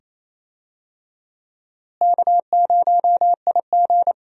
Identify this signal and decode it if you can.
Morse code, heard as a keyed tone, about 28 wpm, 700 Hz. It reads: K0IG
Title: USA Call Signs - 4 Characters - Repeated 2x - 28wpm